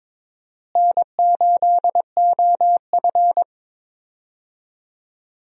Morse code, heard as a keyed tone, about 22 wpm, 700 Hz. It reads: N8OF
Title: USA Call Signs - 4 Characters - Repeated 3x - 22wpm